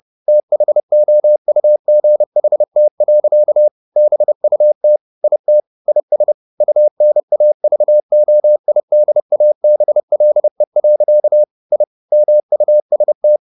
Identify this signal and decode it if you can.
Morse code, heard as a keyed tone, about 30 wpm, 600 Hz. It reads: THOUGHT. BUT IT IS UNAVOIDABLE. I MUST
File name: the old man and the sea / the old man and the sea 1 30wpm